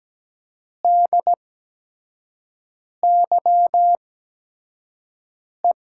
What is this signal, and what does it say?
Morse code, heard as a keyed tone, about 17 wpm, spaced as for 5 wpm, 700 Hz. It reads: DYE
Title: USA Call Signs - 5 Characters - 8x Character Spacing - 17wpm